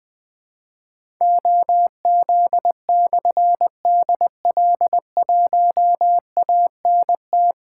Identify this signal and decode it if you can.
Morse code, heard as a keyed tone, about 20 wpm, 700 Hz. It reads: OZ/DL1ANT
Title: Difficult Call Signs 20wpm